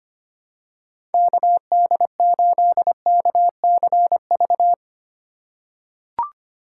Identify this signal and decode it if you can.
Morse code, heard as a keyed tone, about 25 wpm, 700 Hz. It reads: KD8KCV
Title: USA Call Signs - 2x Character Spacing - 25wpm